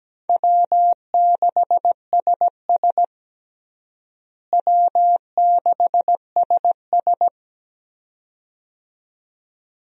Morse code, heard as a keyed tone, about 17 wpm, 700 Hz. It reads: W6SS W6SS
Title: USA Call Signs - 4 Characters - Repeated 2x - 17wpm